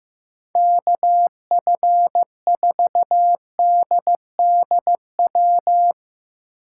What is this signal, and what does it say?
Morse code, heard as a keyed tone, about 15 wpm, 700 Hz. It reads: KF4DDW
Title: USA Call Signs - 6 Characters - 3x Character Spacing - 15wpm